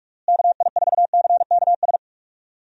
Morse code, heard as a keyed tone, about 43 wpm, 700 Hz. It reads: KI4CKS
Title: USA Call Signs - 6 Characters - Repeated 3x - 45wpm